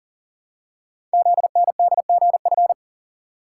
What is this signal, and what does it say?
Morse code, heard as a keyed tone, about 40 wpm, 690 Hz. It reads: ZNDGF
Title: Code Groups - 5 Letters - 40wpm